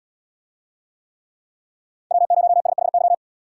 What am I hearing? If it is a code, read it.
Morse code, heard as a keyed tone, about 75 wpm, 700 Hz. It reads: W0SHY